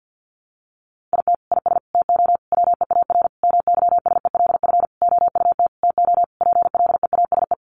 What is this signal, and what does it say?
Morse code, heard as a keyed tone, about 50 wpm, 700 Hz. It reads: IT IS TO WEAR MYSELF OUT TO PLEASE
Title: Sentences from Top 1500 Words 50wpm